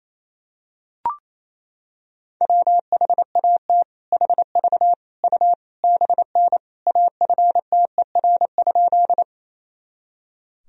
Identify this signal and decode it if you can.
Morse code, heard as a keyed tone, about 28 wpm, 700 Hz. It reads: WHAT HV U BN AFTER?
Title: Sentences from Top 200 Words and Abbreviations 28wpm